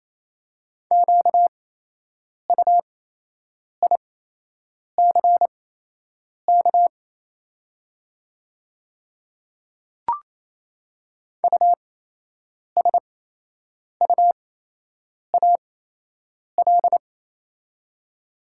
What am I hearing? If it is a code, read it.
Morse code, heard as a keyed tone, about 28 wpm, spaced as for 8 wpm, 700 Hz. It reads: QUICK USUAL
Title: 8x Character Spacing - Top 500 Words - 28wpm